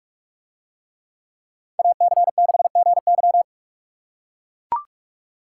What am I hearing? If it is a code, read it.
Morse code, heard as a keyed tone, about 44 wpm, 700 Hz. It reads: AC6GY